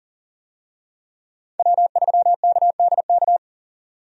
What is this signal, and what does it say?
Morse code, heard as a keyed tone, about 37 wpm, 700 Hz. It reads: W3KDK
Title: USA Call Signs - 5 Characters - 40wpm